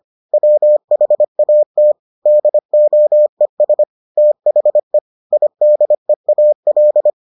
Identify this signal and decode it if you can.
Morse code, heard as a keyed tone, about 25 wpm, 600 Hz. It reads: WHAT DOES THE IDEAL